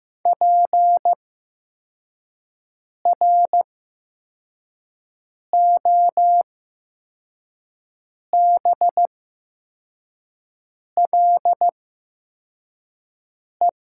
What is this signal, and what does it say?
Morse code, heard as a keyed tone, about 15 wpm, spaced as for 4 wpm, 700 Hz. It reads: PROBLE
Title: 8x Character Spacing - Top 500 Words - 15wpm